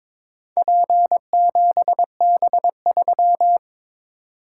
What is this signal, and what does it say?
Morse code, heard as a keyed tone, about 22 wpm, 700 Hz. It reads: P7B3
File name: practice / 22wpm / Four Letter-Number Combinations - 22wpm